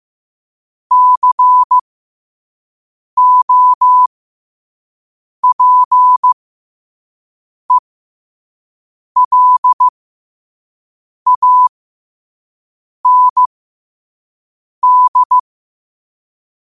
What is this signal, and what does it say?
Morse code, heard as a keyed tone, about 15 wpm, spaced as for 5 wpm, 1000 Hz. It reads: COPELAND
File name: qotdfiles / 5wpm